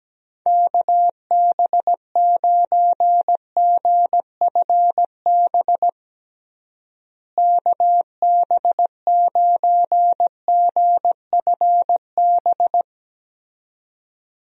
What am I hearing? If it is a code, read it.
Morse code, heard as a keyed tone, about 17 wpm, 700 Hz. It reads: KB9GFB KB9GFB
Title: USA Call Signs - 6 Characters - Repeated 2x - 17wpm